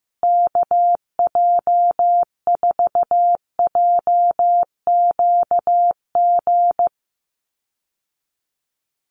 Morse code, heard as a keyed tone, about 15 wpm, 700 Hz. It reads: KJ4JQG
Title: USA Call Signs Repeated 3x 15wpm